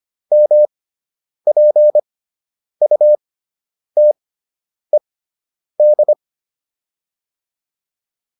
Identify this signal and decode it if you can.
Morse code, heard as a keyed tone, about 25 wpm, 600 Hz. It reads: M P U T E D